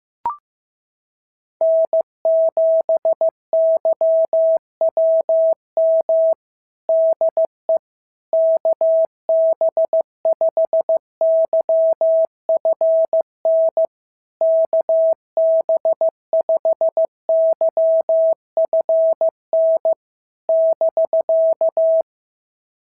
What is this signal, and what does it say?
Morse code, heard as a keyed tone, about 15 wpm, 650 Hz. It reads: N7YWM DE KB5YFN KB5YFN <BK>
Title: SOTA QSO Practice 15wpm